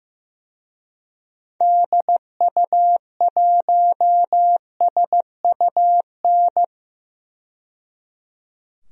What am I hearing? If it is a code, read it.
Morse code, heard as a keyed tone, about 15 wpm, 700 Hz. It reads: DU1SUN
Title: DXCC-100 Call Signs - 6 Characters - 15wpm